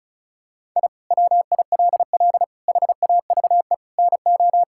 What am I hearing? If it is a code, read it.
Morse code, heard as a keyed tone, about 34 wpm, 700 Hz. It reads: I WILL HAVE NO
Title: Sentences from Top 1000 Words - Review Part 5 - 35wpm